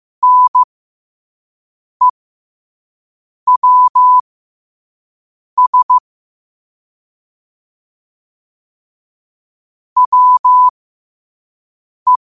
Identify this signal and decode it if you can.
Morse code, heard as a keyed tone, about 15 wpm, spaced as for 5 wpm, 1000 Hz. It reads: NEWS WE
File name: qotdfiles / 5wpm